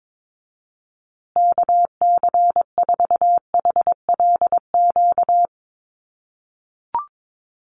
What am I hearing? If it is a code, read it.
Morse code, heard as a keyed tone, about 22 wpm, 700 Hz. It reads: KC4HLQ
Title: USA Call Signs Repeated 3x 22wpm